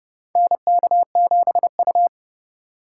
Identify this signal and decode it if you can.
Morse code, heard as a keyed tone, about 30 wpm, 700 Hz. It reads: NK7U
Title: K1USN SST QSO Practice - 30wpm